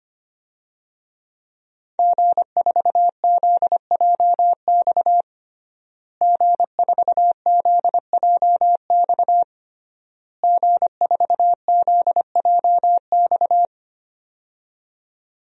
Morse code, heard as a keyed tone, about 25 wpm, 700 Hz. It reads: G4ZJX G4ZJX G4ZJX
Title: DXCC-100 Call Signs - 5 Characters - Repeated 3x - 25wpm